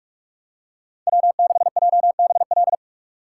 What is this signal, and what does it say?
Morse code, heard as a keyed tone, about 41 wpm, 700 Hz. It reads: W6JBL